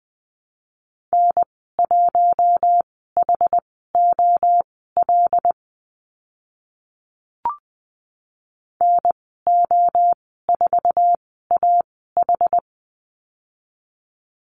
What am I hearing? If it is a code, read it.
Morse code, heard as a keyed tone, about 20 wpm, spaced as for 14 wpm, 700 Hz. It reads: N1HOL NO4AH
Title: 082 Lesson 14g - Number 2 - Call Signs 20wpm